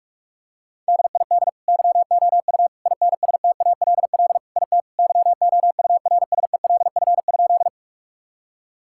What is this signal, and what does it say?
Morse code, heard as a keyed tone, about 42 wpm, 700 Hz. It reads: DID YOU INSTALL IT YOURSELF?
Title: Sentences from Top 1600 Words 45wpm